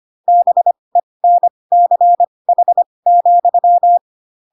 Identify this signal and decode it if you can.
Morse code, heard as a keyed tone, about 25 wpm, spaced as for 20 wpm, 700 Hz. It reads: BENCH,
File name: Pinocchio / Pinocchio ch4 15 25 ik1hge